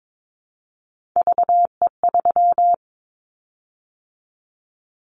Morse code, heard as a keyed tone, about 22 wpm, 700 Hz. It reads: VE3
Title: Callsign Prefixes for Top 25 DXCC - 22wpm